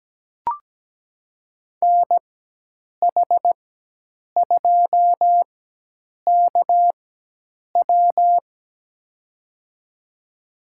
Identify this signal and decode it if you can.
Morse code, heard as a keyed tone, about 17 wpm, spaced as for 8 wpm, 700 Hz. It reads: NH2KW
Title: USA Call Signs - 4x Character Spacing - 17wpm